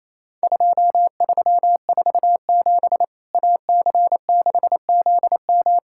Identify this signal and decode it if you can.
Morse code, heard as a keyed tone, about 28 wpm, 700 Hz. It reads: 2347 AC6ZM
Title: Worked All Europe DX Contest - QTC Exchange - 28wpm